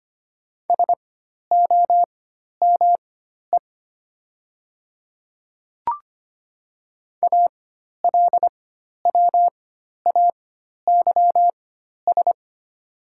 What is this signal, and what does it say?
Morse code, heard as a keyed tone, about 25 wpm, spaced as for 12 wpm, 700 Hz. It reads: SOME ALWAYS
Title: 4x Character Spacing - Top 300 Words - 25wpm